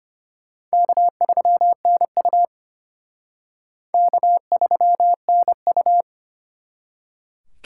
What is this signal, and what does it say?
Morse code, heard as a keyed tone, about 30 wpm, 700 Hz. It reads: K3NU K3NU
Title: Speed Racing - USA 1x2 Call Signs 25wpm